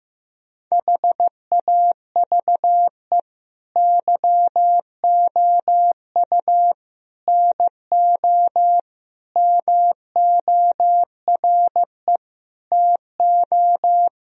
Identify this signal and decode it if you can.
Morse code, heard as a keyed tone, about 15 wpm, 700 Hz. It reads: HAVE YOU NO MORE TO
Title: Sentences from Top 1000 Words - Review Part 1 - 15wpm